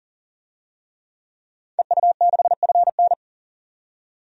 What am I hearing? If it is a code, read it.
Morse code, heard as a keyed tone, about 40 wpm, 700 Hz. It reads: EU6FN
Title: DXCC-100 Call Signs - 5 Characters - 40wpm